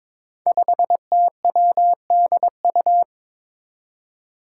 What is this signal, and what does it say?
Morse code, heard as a keyed tone, about 22 wpm, 700 Hz. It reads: 5TWDU
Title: Five Challenging Letter-Number Combinations 22wpm